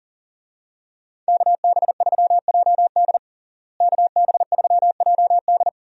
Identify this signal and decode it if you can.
Morse code, heard as a keyed tone, about 40 wpm, 700 Hz. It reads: KB3JD KB3JD
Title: USA Call Signs - 5 Characters - Repeated 3x - 40wpm